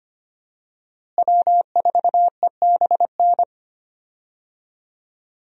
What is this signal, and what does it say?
Morse code, heard as a keyed tone, about 25 wpm, 700 Hz. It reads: W4EBN
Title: USA Call Signs - 5 Characters - Repeated 2x - 25wpm